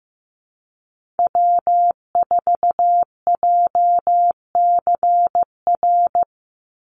Morse code, heard as a keyed tone, about 15 wpm, 700 Hz. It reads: W4JCR
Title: USA Call Signs 15wpm